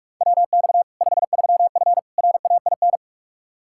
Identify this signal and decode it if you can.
Morse code, heard as a keyed tone, about 42 wpm, 700 Hz. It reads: WX 53F RAIN
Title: Practice QSO - Exchange 2 - Info - 45wpm